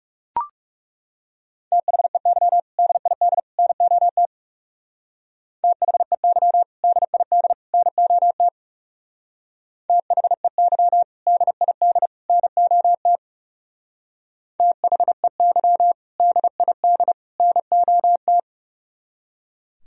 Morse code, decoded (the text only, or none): THEY DID NOT THEY DID NOT THEY DID NOT THEY DID NOT